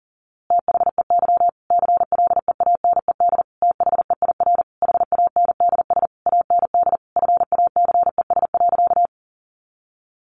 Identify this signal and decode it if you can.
Morse code, heard as a keyed tone, about 40 wpm, 700 Hz. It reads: THEY CLEANED THEIR HANDS AND FACES.
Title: A Sound Of Thunder 40wpm